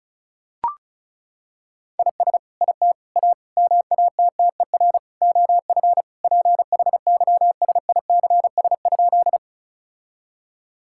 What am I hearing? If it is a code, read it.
Morse code, heard as a keyed tone, about 34 wpm, 700 Hz. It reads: IS IT A MATTER OF PHYSICS?
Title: Sentences from Top 1000 Words - Review Part 5 - 35wpm